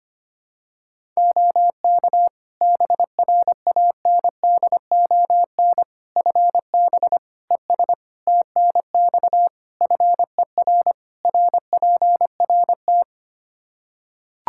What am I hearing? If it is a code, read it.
Morse code, heard as a keyed tone, about 25 wpm, 700 Hz. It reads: OK BRANDON FB ES TNX FER RPRT
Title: K1USN SST QSO Practice 25wpm